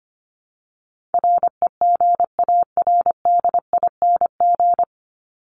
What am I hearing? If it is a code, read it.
Morse code, heard as a keyed tone, about 25 wpm, 700 Hz. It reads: REGARDING
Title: ING Suffix - Top 100 Words with Base Words 25wpm